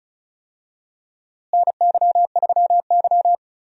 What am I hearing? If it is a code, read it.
Morse code, heard as a keyed tone, about 34 wpm, 700 Hz. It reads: NY3Y
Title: USA Call Signs - 4 Characters - Repeated 3x - 35wpm